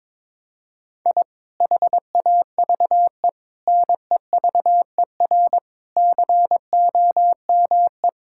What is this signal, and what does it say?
Morse code, heard as a keyed tone, about 22 wpm, 700 Hz. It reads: I HAVE NEVER COME
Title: Sentences from Top 1000 Words - Review Part 5 - 22wpm